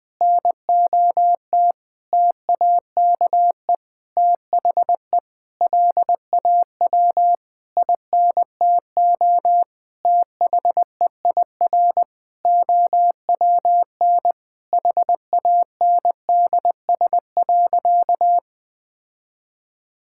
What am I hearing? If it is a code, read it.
Morse code, heard as a keyed tone, about 20 wpm, 700 Hz. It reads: NOT TAKE THE LAW INTO THEIR OWN HANDS.
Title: Sentences with 8-Word Phrases - Spoken Beginning - 20wpm